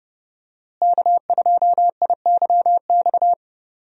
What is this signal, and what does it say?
Morse code, heard as a keyed tone, about 30 wpm, 700 Hz. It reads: K2IYX
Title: USA Call Signs - 5 Characters - Repeated 2x - 30wpm